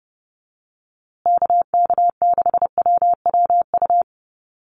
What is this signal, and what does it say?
Morse code, heard as a keyed tone, about 29 wpm, 700 Hz. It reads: KK6WWU
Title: USA Call Signs Repeated 3x 30wpm